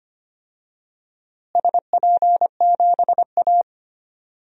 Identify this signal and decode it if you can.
Morse code, heard as a keyed tone, about 25 wpm, 700 Hz. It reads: SP7A
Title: DXCC-100 Call Signs - 4 Characters - Repeated 2x - 25wpm